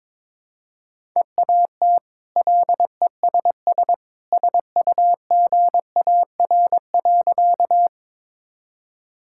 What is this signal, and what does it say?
Morse code, heard as a keyed tone, about 22 wpm, 700 Hz. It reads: EAT LESS SUGAR.
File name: practice / 22wpm / Easy Sentences with Hints - 22wpm